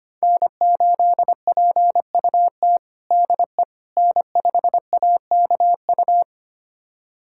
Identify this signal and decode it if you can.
Morse code, heard as a keyed tone, about 25 wpm, 700 Hz. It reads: N8PUT DE N5AKU
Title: USA Callsign DE Callsign - 5 Characters - 25wpm